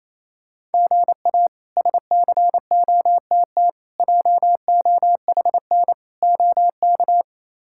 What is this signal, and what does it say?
Morse code, heard as a keyed tone, about 28 wpm, 700 Hz. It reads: GA SCOTT JOHN OK
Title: K1USN SST QSO Practice - 28wpm